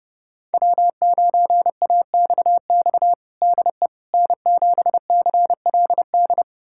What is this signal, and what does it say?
Morse code, heard as a keyed tone, about 30 wpm, 700 Hz. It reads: W9AXX DE N7CLD
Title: USA Callsign DE Callsign - 5 Characters - 30wpm